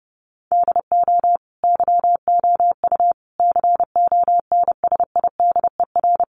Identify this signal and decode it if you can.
Morse code, heard as a keyed tone, about 30 wpm, 700 Hz. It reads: DO YOU CONSIDER
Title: Sentences from Top 600 Words 30wpm